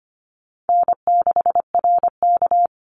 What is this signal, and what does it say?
Morse code, heard as a keyed tone, about 25 wpm, 700 Hz. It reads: N6RK
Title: Easy Call Signs - 25wpm